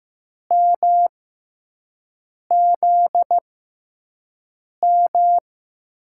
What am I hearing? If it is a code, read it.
Morse code, heard as a keyed tone, about 15 wpm, spaced as for 5 wpm, 700 Hz. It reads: MZM
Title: USA Call Signs - 6 Characters - 6x Character Spacing - 15wpm